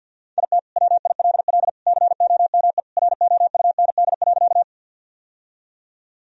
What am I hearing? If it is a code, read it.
Morse code, heard as a keyed tone, about 45 wpm, 700 Hz. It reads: IT WILL COME ROUND.
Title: Sentences from Top 1000 Words - Review Part 1 - 50wpm